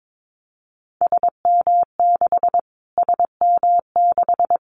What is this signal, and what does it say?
Morse code, heard as a keyed tone, about 22 wpm, 700 Hz. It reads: SM6 SM6
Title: Callsign Prefixes for Top 25 DXCC - Repeated 3x - 22wpm